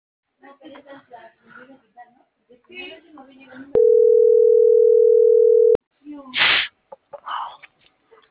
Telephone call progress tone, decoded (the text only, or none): ringback tone